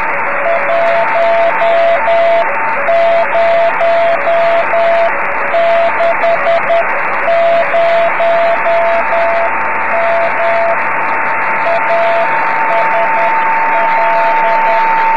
Morse code, heard as a keyed tone, about 10 wpm, 680 Hz, with radio noise. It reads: 1060M ASL
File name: om3pv 70mhz / zaznam / z35msk